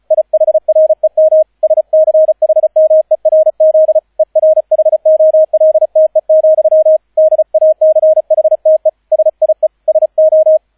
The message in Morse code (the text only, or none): IHREM SCHMERZ ERHOLTE, DACHTE SIE SO